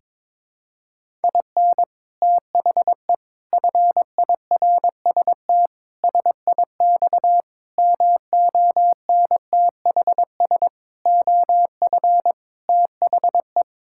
Morse code, heard as a keyed tone, about 21 wpm, 700 Hz. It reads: IN THE FIRST SIX MONTHS OF THE
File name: practice / 22wpm / Sets of 8 Words - 22wpm